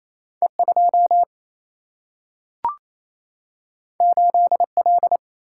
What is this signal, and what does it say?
Morse code, heard as a keyed tone, about 28 wpm, 700 Hz. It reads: E 2 8 L
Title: Single-Letter Single-Number 28wpm